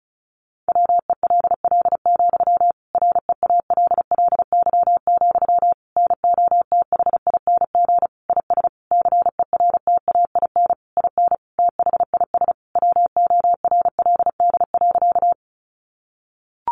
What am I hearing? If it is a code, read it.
Morse code, heard as a keyed tone, about 34 wpm, 700 Hz. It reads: WELL, REALLY, NOTHING IS CERTAIN IN THIS WORLD.
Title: Sentences from Top 600 Words 35wpm